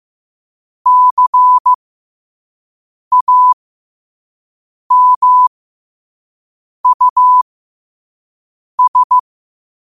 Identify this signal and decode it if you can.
Morse code, heard as a keyed tone, about 15 wpm, spaced as for 5 wpm, 1000 Hz. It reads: CAMUS
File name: qotdfiles / 5wpm